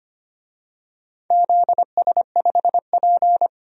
Morse code, heard as a keyed tone, about 25 wpm, 700 Hz. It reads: ZS5P
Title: DXCC-100 Call Signs - 4 Characters - 25wpm